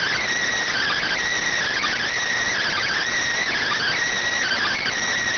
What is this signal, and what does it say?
Emergency Alert System (SAME) tones (header bursts), heard as harsh buzzing bursts. It is heard from 0.7 to 4.9 s.